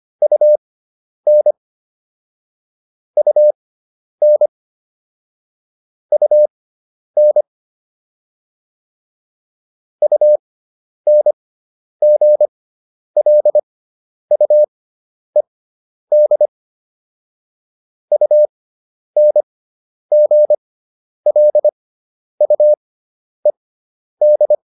Morse code, heard as a keyed tone, about 25 wpm, spaced as for 10 wpm, 600 Hz. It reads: UN UN UN UNGLUED UNGLUED